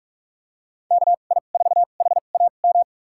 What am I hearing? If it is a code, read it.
Morse code, heard as a keyed tone, about 46 wpm, spaced as for 33 wpm, 700 Hz. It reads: KI4HAM
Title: USA Call Signs - 6 Characters - 2x Character Spacing - 45wpm